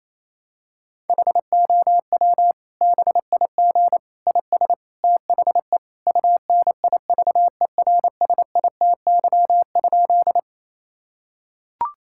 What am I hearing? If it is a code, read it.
Morse code, heard as a keyed tone, about 28 wpm, 700 Hz. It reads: HOW BIG IS THE UNIVERSITY?
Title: Sentences from Top 1000 Words - Review Part 5 - 28wpm